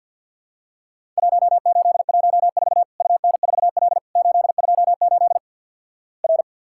Morse code, heard as a keyed tone, about 50 wpm, 700 Hz. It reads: 1814 UN4L 828 R
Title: Worked All Europe DX Contest - QTC Exchange with Repeats - 50wpm